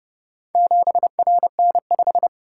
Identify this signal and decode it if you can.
Morse code, heard as a keyed tone, about 28 wpm, 700 Hz. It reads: 7RN5